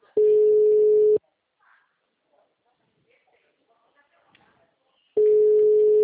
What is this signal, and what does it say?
Ringback tone (Europe, 425 Hz), with speech.
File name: out- -2001- - - .2377